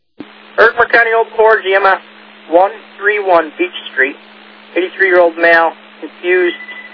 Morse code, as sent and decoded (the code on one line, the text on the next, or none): none